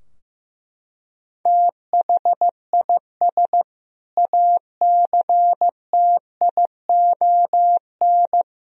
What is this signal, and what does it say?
Morse code, heard as a keyed tone, about 15 wpm, 700 Hz. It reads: THIS ACTION
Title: Sentences from Top 1000 Words - Review Part 4 - 15wpm